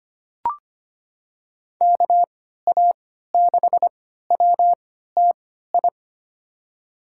Morse code, heard as a keyed tone, about 25 wpm, spaced as for 14 wpm, 700 Hz. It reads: KA6WTI
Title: USA Call Signs - 6 Characters - 3x Character Spacing - 25wpm